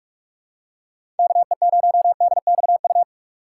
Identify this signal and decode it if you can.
Morse code, heard as a keyed tone, about 43 wpm, 700 Hz. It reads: KE0DXU
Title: USA Call Signs - 2x Character Spacing - 45wpm